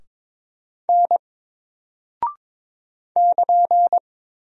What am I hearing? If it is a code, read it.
Morse code, heard as a keyed tone, about 22 wpm, 700 Hz. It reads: N <KN>
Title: Single Character (Letter:Number:Punctuation:Prosign) 22wpm